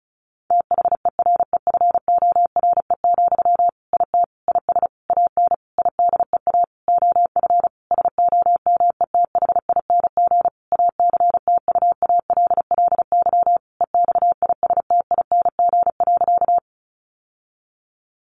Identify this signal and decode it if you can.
Morse code, heard as a keyed tone, about 34 wpm, 700 Hz. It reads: THEREFORE, IT IS AN IDEA OF SOMETHING ACTUALLY EXISTING.
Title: Sentences from Top 1500 Words 35wpm